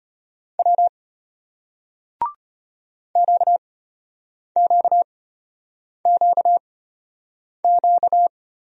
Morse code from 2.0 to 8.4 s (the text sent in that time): QQQQ